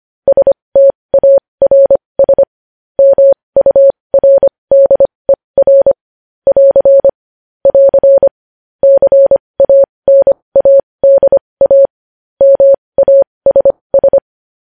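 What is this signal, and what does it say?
Morse code, heard as a keyed tone, about 25 wpm, spaced as for 20 wpm, 550 Hz. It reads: STARS MURDER <AR> <AR> CANADA MASS
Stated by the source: Morse code news headlines sent at 20 wpm.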